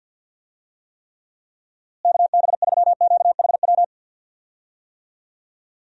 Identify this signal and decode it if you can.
Morse code, heard as a keyed tone, about 46 wpm, 700 Hz. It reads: KB3QHW
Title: USA Call Signs - 6 Characters - Repeated 2x - 50wpm